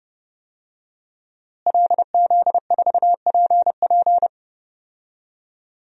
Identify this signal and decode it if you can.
Morse code, heard as a keyed tone, about 32 wpm, 700 Hz. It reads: LZ4PP
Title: DXCC-100 Call Signs - 5 Characters - Repeated 3x - 30wpm